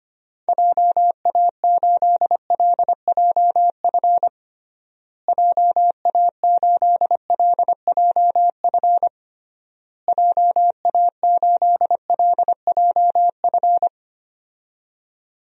Morse code, heard as a keyed tone, about 25 wpm, 700 Hz. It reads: JA8LJF JA8LJF JA8LJF
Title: DXCC-100 Call Signs - 6 Characters - Repeated 3x - 25wpm